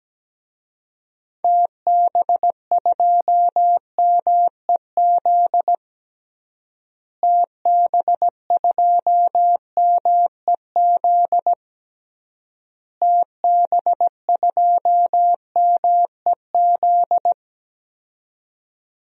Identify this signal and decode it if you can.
Morse code, heard as a keyed tone, about 17 wpm, 700 Hz. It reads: TB2MEZ TB2MEZ TB2MEZ
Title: DXCC-100 Call Signs - 6 Characters - Repeated 3x - 17wpm